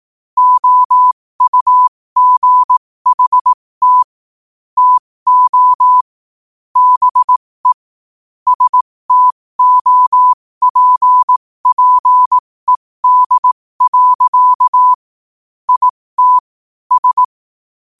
Morse code, heard as a keyed tone, about 18 wpm, spaced as for 15 wpm, 1000 Hz. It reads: OUGHT TO BE STOPPED. IT S
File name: qotdfiles / 15wpm